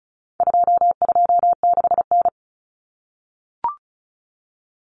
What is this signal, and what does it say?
Morse code, heard as a keyed tone, about 34 wpm, 700 Hz. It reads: 226N
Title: Common Cut Numbers - 4-Digit 35wpm